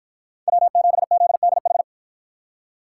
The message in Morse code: W7ZDH

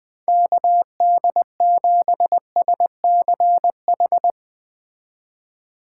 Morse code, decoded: KD7SCH

